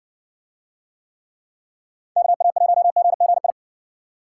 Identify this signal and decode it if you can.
Morse code, heard as a keyed tone, about 60 wpm, 700 Hz. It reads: KA1GGI